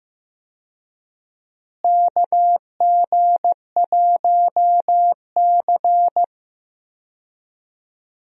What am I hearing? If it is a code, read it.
Morse code, heard as a keyed tone, about 15 wpm, 700 Hz. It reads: KG1C
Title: USA Call Signs - 4 Characters - Repeated 3x - 15wpm